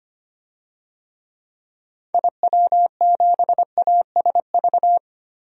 Morse code, heard as a keyed tone, about 25 wpm, 700 Hz. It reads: IW7ASV
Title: DXCC-100 Call Signs - 6 Characters - Repeated 2x - 25wpm